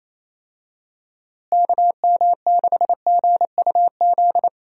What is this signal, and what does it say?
Morse code, heard as a keyed tone, about 28 wpm, 700 Hz. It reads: KM6GUZ
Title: USA Call Signs - 6 Characters - Repeated 2x - 28wpm